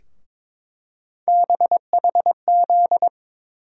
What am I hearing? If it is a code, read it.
Morse code, heard as a keyed tone, about 22 wpm, 700 Hz. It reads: BHZ
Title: USA Call Sign Postfixes - 3 Letters 22wpm